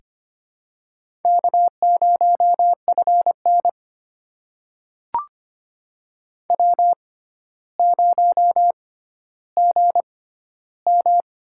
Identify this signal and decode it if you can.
Morse code, heard as a keyed tone, about 25 wpm, spaced as for 13 wpm, 700 Hz. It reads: K0FN W0GM
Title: USA Call Signs - 4 Characters - 6x Character Spacing - 25wpm